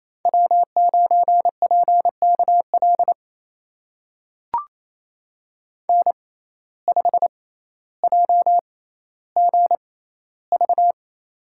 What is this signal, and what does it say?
Morse code, heard as a keyed tone, about 28 wpm, spaced as for 14 wpm, 700 Hz. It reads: W9PKL N5JGV
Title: USA Call Signs - 6x Character Spacing - 28wpm